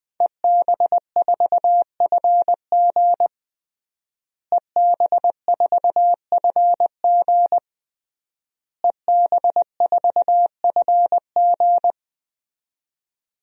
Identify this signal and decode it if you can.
Morse code, heard as a keyed tone, about 20 wpm, 700 Hz. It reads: EB4FG EB4FG EB4FG
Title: DXCC-100 Call Signs - 5 Characters - Repeated 3x - 20wpm